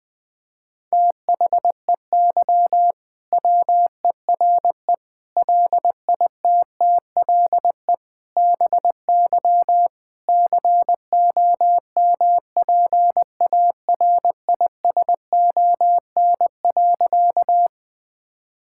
Morse code, heard as a keyed tone, about 20 wpm, 700 Hz. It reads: THEY WERE LITTLE BY COMPARISON.
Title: Sentences from Top 1600 Words 20wpm